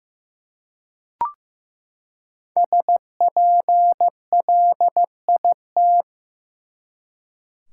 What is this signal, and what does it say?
Morse code, heard as a keyed tone, about 15 wpm, 700 Hz. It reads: SPLIT
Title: Five Letter Words - 15wpm